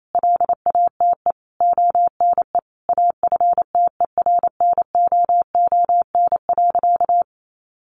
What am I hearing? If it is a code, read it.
Morse code, heard as a keyed tone, about 28 wpm, 700 Hz. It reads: LATE ONE AFTERNOON.
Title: Sentences from Top 1500 Words 28wpm